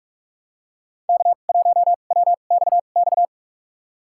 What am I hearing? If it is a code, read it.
Morse code, heard as a keyed tone, about 44 wpm, spaced as for 32 wpm, 690 Hz. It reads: K1WXX
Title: USA Call Signs - 5 Characters - 2x Character Spacing - 45wpm